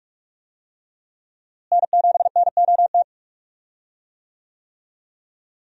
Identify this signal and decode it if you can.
Morse code, heard as a keyed tone, about 46 wpm, 700 Hz. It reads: N7NOT
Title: USA Call Signs - 5 Characters - Repeated 3x - 45wpm